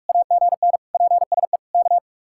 Morse code, heard as a keyed tone, about 45 wpm, 700 Hz. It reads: AGN PSE K